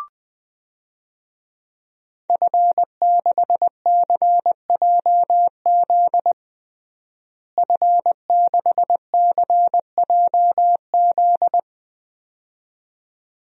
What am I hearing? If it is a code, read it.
Morse code, heard as a keyed tone, about 20 wpm, 700 Hz. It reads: F6CJZ F6CJZ
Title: DXCC-100 Call Signs - 5 Characters - Repeated 2x - 20wpm